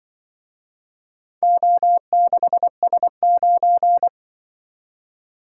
Morse code, heard as a keyed tone, about 24 wpm, 700 Hz. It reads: O6S9
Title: Speed-Racing - Four Letter-Number Combinations - 20wpm